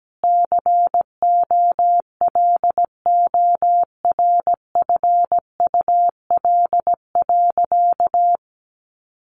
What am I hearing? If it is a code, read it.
Morse code, heard as a keyed tone, about 17 wpm, 700 Hz. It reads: COLORFUL.
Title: Easy Phrases 17wpm